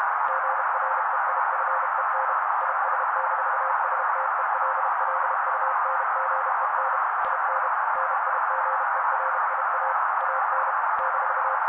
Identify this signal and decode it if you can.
Morse code, heard as a keyed tone, about 32 wpm, 530 Hz, with noise. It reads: GREIFEN IS-STELLUNGEN IN NIGERIA AN -
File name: daily / 31wpm noise